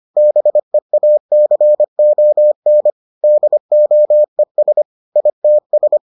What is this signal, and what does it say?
Morse code, heard as a keyed tone, about 25 wpm, 600 Hz. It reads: BEACON DOES ITS